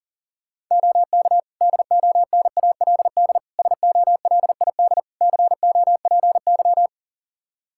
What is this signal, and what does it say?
Morse code, heard as a keyed tone, about 40 wpm, 700 Hz. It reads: OK DONALD SOLID COPY
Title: Practice QSO - Exchange 3 - More Info - 40wpm